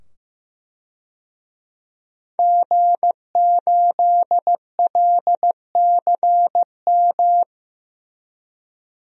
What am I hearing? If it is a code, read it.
Morse code, heard as a keyed tone, about 15 wpm, 700 Hz. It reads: G8LCM
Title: DXCC-100 Call Signs - 5 Characters - 15wpm